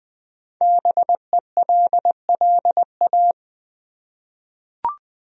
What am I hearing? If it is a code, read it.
Morse code, heard as a keyed tone, about 20 wpm, 700 Hz. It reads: BELLA
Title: USA Common Names - Max 5 Letters - 20wpm